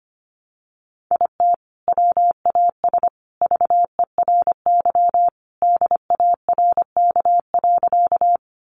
Morse code, heard as a keyed tone, about 25 wpm, 700 Hz. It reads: IT WAS VERY DARK.
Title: Sentences from Top 500 Words 25wpm